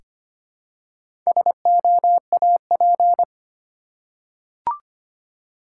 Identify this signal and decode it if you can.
Morse code, heard as a keyed tone, about 25 wpm, 690 Hz. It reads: SOAP
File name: practice / 25wpm / Easy Words with Hints - 25wpm